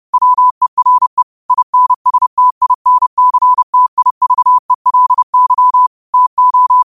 Morse code, heard as a keyed tone, about 30 wpm, 1000 Hz. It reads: WERE INSTINCTIVELY TO